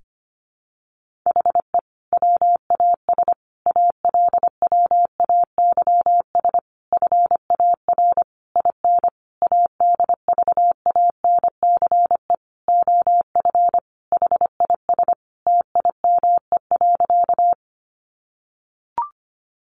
Morse code, heard as a keyed tone, about 25 wpm, 700 Hz. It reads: HE WAS ALWAYS FAR IN ADVANCE OF HIS TIME.
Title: Sentences from Top 1400 Words 25wpm